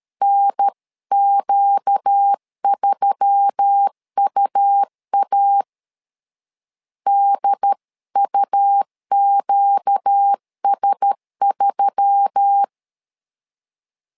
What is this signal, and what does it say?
Morse code, heard as a keyed tone, about 13 wpm, 800 Hz. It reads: NQ3UA DUQS3
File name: LEZIONI / 4 Q1S3 12wpm 2023